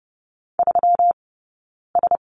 Morse code, heard as a keyed tone, about 30 wpm, 700 Hz. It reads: 3 S
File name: Lesson 20d - Number 3 - Two Characters 30wpm